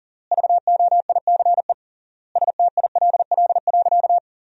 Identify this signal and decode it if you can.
Morse code, heard as a keyed tone, about 40 wpm, 700 Hz. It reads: VOICE STILL.
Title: 3x Word-Spacing - Sentences from Top 500 Words - 40wpm